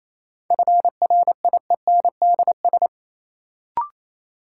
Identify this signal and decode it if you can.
Morse code, heard as a keyed tone, about 28 wpm, 700 Hz. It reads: FRIENDS
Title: Sentences from Top 1600 Words - New Word Drill Alt - 28wpm